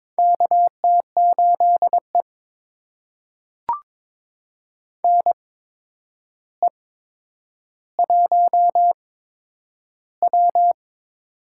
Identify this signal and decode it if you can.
Morse code, heard as a keyed tone, about 22 wpm, spaced as for 9 wpm, 700 Hz. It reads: KT8E NE1W
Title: USA Call Signs - 4 Characters - 8x Character Spacing - 22wpm